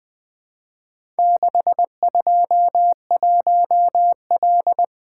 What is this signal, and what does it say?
Morse code, heard as a keyed tone, about 20 wpm, 700 Hz. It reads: 621L
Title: Four Letter-Number Combinations - 20wpm